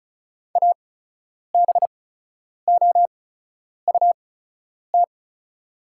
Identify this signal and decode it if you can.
Morse code, heard as a keyed tone, about 36 wpm, spaced as for 10 wpm, 700 Hz. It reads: ABOUT
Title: 8x Character Spacing - Top 200 Words - 35wpm